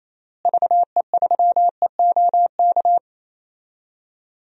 Morse code, heard as a keyed tone, about 28 wpm, 700 Hz. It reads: VE3EOK
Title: DXCC-100 Call Signs - 6 Characters - Repeated 3x - 28wpm